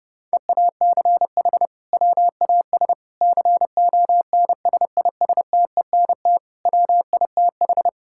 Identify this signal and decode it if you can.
Morse code, heard as a keyed tone, about 32 wpm, 700 Hz. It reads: EACH WAS CONSISTENT WITH
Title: Sentences from Top 1600 Words 30wpm